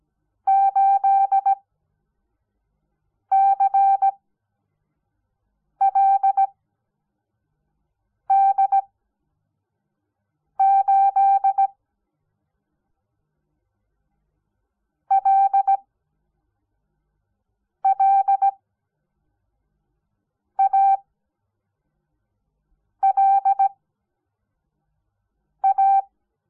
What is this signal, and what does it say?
Morse code, heard as a keyed tone, about 17 wpm, spaced as for 4 wpm, 780 Hz. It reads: 8CLD8 LLALA
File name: Onlinelehrgang / morsen / DJ4UF Morsen06